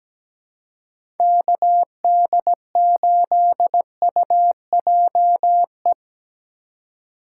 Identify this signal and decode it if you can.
Morse code, heard as a keyed tone, about 17 wpm, 700 Hz. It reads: KD8UJE
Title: USA Call Signs - 6 Characters - 2x Character Spacing - 17wpm